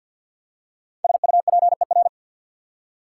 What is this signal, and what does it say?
Morse code, heard as a keyed tone, about 48 wpm, 690 Hz. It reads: SUPER
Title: Prefixes and Suffixes 50wpm